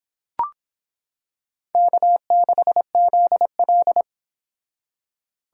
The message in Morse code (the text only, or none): K6ZL